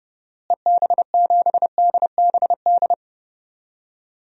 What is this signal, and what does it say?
Morse code, heard as a keyed tone, about 30 wpm, 700 Hz. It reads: EB7DBD
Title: DXCC-100 Call Signs - 6 Characters - Repeated 3x - 30wpm